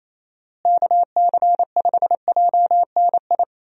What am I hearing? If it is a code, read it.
Morse code, heard as a keyed tone, about 28 wpm, 700 Hz. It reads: KC5JNI
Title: USA Call Signs - 6 Characters - Repeated 3x - 28wpm